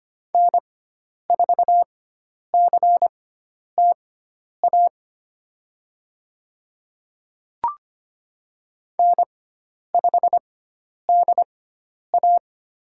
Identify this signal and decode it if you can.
Morse code, heard as a keyed tone, about 25 wpm, spaced as for 10 wpm, 700 Hz. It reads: N4CTA N5DA
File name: Lesson 16g - Letter c - Call Signs 25wpm